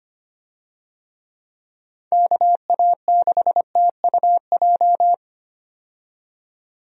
Morse code, heard as a keyed tone, about 25 wpm, 700 Hz. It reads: KA6TUJ
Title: USA Call Signs - 6 Characters - 25wpm